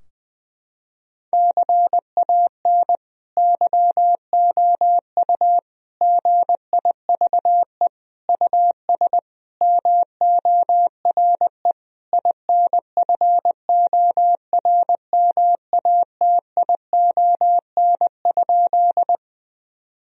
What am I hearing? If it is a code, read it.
Morse code, heard as a keyed tone, about 20 wpm, 700 Hz. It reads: CAN YOU GIVE US MORE INFORMATION?
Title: Sentences from Top 1000 Words - Review Part 3 - 20wpm